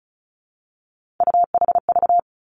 Morse code, heard as a keyed tone, about 36 wpm, 700 Hz. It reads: UHV
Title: Three Challenging Letter-Number Combinations 35wpm